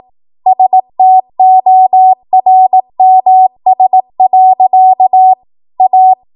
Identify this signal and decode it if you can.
Morse code, heard as a keyed tone, about 18 wpm, 750 Hz. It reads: STORMS. A
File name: Morse / 18WPM